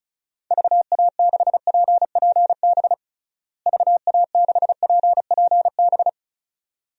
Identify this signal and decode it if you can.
Morse code, heard as a keyed tone, about 36 wpm, 700 Hz. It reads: VA6PPB VA6PPB
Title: DXCC-100 Call Signs - 6 Characters - Repeated 2x - 35wpm